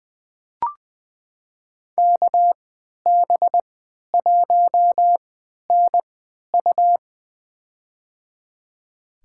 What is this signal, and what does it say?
Morse code, heard as a keyed tone, about 20 wpm, spaced as for 11 wpm, 700 Hz. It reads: KB1NU
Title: USA Call Signs - 3x Character Spacing - 20wpm